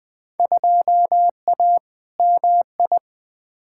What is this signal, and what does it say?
Morse code, heard as a keyed tone, about 20 wpm, 700 Hz. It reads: 2A MI
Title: Field Day Exchanges - 20wpm